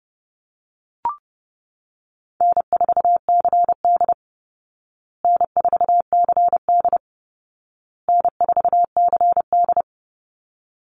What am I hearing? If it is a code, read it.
Morse code, heard as a keyed tone, about 30 wpm, 700 Hz. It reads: N4CD N4CD N4CD
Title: Call Signs Repeated 2x - 30wpm